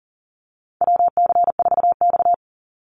Morse code, heard as a keyed tone, about 40 wpm, 700 Hz. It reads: WC4X
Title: Call Signs - 40wpm